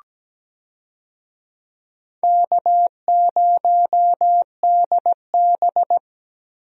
Morse code, heard as a keyed tone, about 17 wpm, 700 Hz. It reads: K0DB
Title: USA Call Signs - 4 Characters - 17wpm